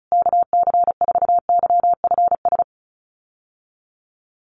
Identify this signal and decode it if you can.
Morse code, heard as a keyed tone, about 34 wpm, 700 Hz. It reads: KC4YFS
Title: USA Call Signs Repeated 3x 35wpm